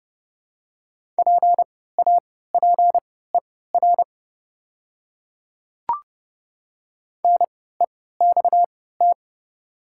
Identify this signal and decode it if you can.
Morse code, heard as a keyed tone, about 30 wpm, spaced as for 17 wpm, 700 Hz. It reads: PAPER NEXT
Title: 3x Character Spacing - Top 300 Words - 30wpm